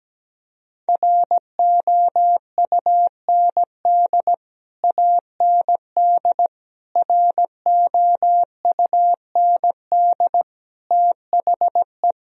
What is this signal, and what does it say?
Morse code, heard as a keyed tone, about 17 wpm, 700 Hz. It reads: ROUND AND ROUND THE
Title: Sentences from Top 1000 Words - Review Part 3 - 17wpm